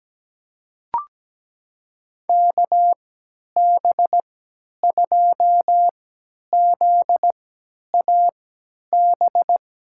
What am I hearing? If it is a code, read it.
Morse code, heard as a keyed tone, about 17 wpm, spaced as for 10 wpm, 700 Hz. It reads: KB2ZAB
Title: USA Call Signs - 6 Characters - 3x Character Spacing - 17wpm